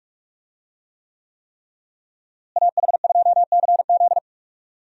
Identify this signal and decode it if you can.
Morse code, heard as a keyed tone, about 46 wpm, 700 Hz. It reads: AH2CZ
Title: USA Call Signs - 5 Characters - Repeated 3x - 45wpm